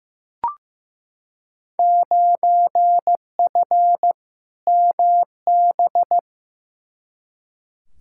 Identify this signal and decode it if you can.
Morse code, heard as a keyed tone, about 15 wpm, 700 Hz. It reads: 9F MB
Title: Field Day Exchanges Encore - 15wpm